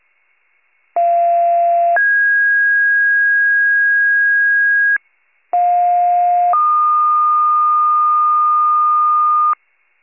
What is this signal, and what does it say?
Two-tone page, heard as a one-second tone, then a three-second tone, about 690 then 1650 Hz, about 710 then 1220 Hz, with noise. It is heard from 1.0 to 9.5 s.
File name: pager / 46 RESCUE / wcalert